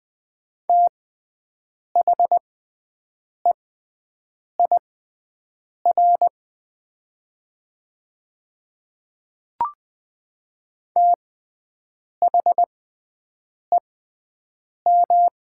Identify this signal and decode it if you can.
Morse code, heard as a keyed tone, about 20 wpm, spaced as for 7 wpm, 700 Hz. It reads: THEIR THEM
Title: 6x Character Spacing - Top 100 Words - 20wpm